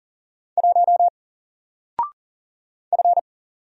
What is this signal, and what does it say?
Morse code, heard as a keyed tone, about 40 wpm, 700 Hz. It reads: 1 F
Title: Single Letter-Number Alt 40wpm